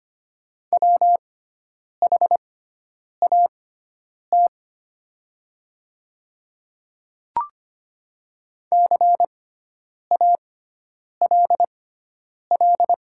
Morse code, heard as a keyed tone, about 25 wpm, spaced as for 9 wpm, 700 Hz. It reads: WHAT CALL
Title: 6x Character Spacing - Top 100 Words - 25wpm